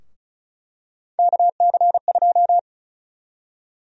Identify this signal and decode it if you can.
Morse code, heard as a keyed tone, about 36 wpm, 700 Hz. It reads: KC2
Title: USA Call Sign Prefixes - 35wpm